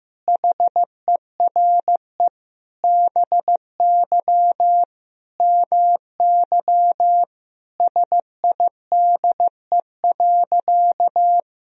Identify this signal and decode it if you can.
Morse code, heard as a keyed tone, about 15 wpm, 700 Hz. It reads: HERE BY MY SIDE.
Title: Sentences from Top 1000 Words - Review Part 1 - 15wpm